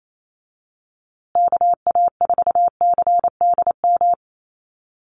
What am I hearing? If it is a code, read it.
Morse code, heard as a keyed tone, about 28 wpm, 700 Hz. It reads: KA4CDM
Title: USA Call Signs Repeated 2x 28wpm